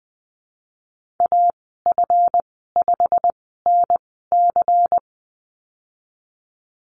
Morse code, heard as a keyed tone, about 20 wpm, spaced as for 15 wpm, 700 Hz. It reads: AF5NC
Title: 147 Lesson 24g - Letter y - Call Signs 20wpm